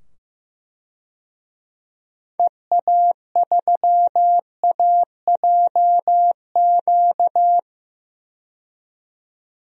Morse code, heard as a keyed tone, about 15 wpm, 700 Hz. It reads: EA3AJQ
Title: DXCC-100 Call Signs - 6 Characters - 15wpm